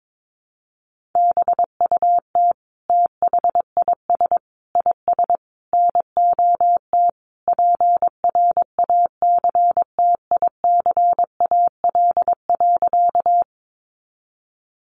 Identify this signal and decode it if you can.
Morse code, heard as a keyed tone, about 22 wpm, 700 Hz. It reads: BUT THIS IS NOT PRACTICAL.
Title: Sentences from Top 1400 Words 22wpm